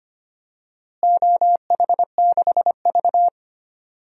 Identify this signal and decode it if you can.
Morse code, heard as a keyed tone, about 25 wpm, 700 Hz. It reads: OH6V
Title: DXCC-100 Call Signs - 4 Characters - Repeated 2x - 25wpm